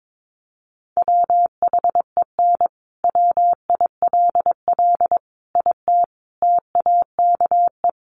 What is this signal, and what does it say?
Morse code, heard as a keyed tone, about 22 wpm, 700 Hz. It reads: WHEN WILL IT TAKE
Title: Sentences from Top 200 Words 22wpm